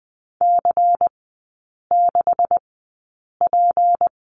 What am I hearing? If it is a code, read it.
Morse code, heard as a keyed tone, about 20 wpm, spaced as for 8 wpm, 700 Hz. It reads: C6P
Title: 218 Lesson 35e - Number 0 - Three Characters 20wpm